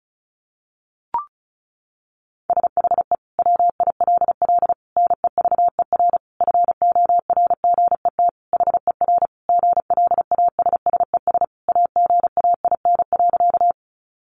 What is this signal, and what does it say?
Morse code, heard as a keyed tone, about 34 wpm, 700 Hz. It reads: SHE WILL NEVER FORGET HER GLASSES AGAIN.
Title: Additional Easy Phrases 35wpm